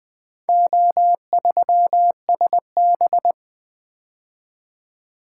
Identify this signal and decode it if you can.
Morse code, heard as a keyed tone, about 20 wpm, 700 Hz. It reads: O3SB
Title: Four Letter-Number Combinations - 20wpm